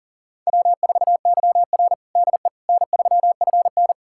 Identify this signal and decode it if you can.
Morse code, heard as a keyed tone, about 37 wpm, 700 Hz. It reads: W4YR DE N3FN
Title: USA Callsign DE Callsign - 4 Characters - 40wpm